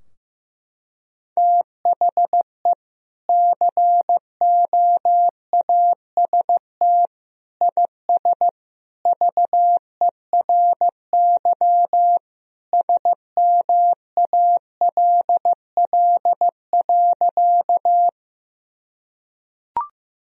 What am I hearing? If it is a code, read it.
Morse code, heard as a keyed tone, about 15 wpm, 700 Hz. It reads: THE COAST IS VERY SMALL.
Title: Sentences from Top 1000 Words - Review Part 4 - 15wpm